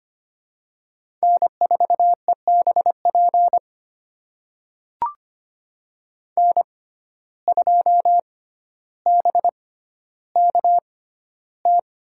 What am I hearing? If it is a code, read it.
Morse code, heard as a keyed tone, about 25 wpm, spaced as for 13 wpm, 700 Hz. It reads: N4EBP N2BKT
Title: USA Call Signs - 5 Characters - 6x Character Spacing - 25wpm